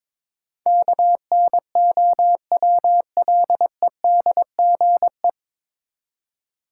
Morse code, heard as a keyed tone, about 22 wpm, 700 Hz. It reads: KNOWLEDGE
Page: Morse Code Ninja practice 22wpm